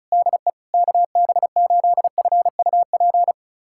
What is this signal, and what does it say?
Morse code, heard as a keyed tone, about 36 wpm, 700 Hz. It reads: DE KB8FUP
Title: USA Callsign DE Callsign 35wpm